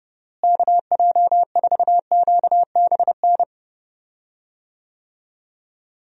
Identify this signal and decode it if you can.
Morse code, heard as a keyed tone, about 30 wpm, 700 Hz. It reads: KJ4QBN
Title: USA Call Signs - 6 Characters - Repeated 2x - 30wpm